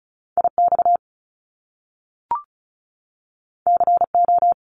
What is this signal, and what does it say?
Morse code, heard as a keyed tone, about 36 wpm, 700 Hz. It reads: IX CO
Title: Top Two Letter Combinations in Words 35wpm